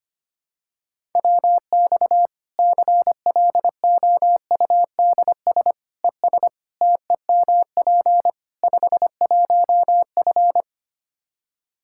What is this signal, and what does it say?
Morse code, heard as a keyed tone, about 25 wpm, 700 Hz. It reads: WX CLOUDS ES TEMP 51F
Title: Practice QSO - Exchange 2 - Info - 25wpm